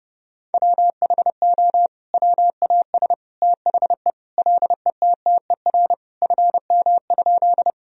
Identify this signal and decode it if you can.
Morse code, heard as a keyed tone, about 30 wpm, 700 Hz. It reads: WHO WAS THE LETTER FM?
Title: Sentences from Top 300 Words and Abbreviations 30wpm